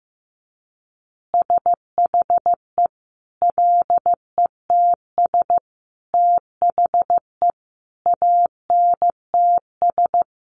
Morse code, heard as a keyed tone, about 15 wpm, 700 Hz. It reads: SHE LETS THE ANTS
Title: Additional Easy Phrases 15wpm